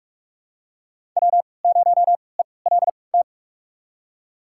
Morse code, heard as a keyed tone, about 46 wpm, spaced as for 25 wpm, 700 Hz. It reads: W0ELT
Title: USA Call Signs - 3x Character Spacing - 45wpm